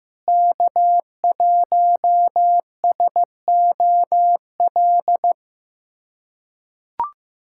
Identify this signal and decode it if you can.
Morse code, heard as a keyed tone, about 16 wpm, 700 Hz. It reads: K1SOL
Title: USA Callsign DE Callsign - 5 Characters - 15wpm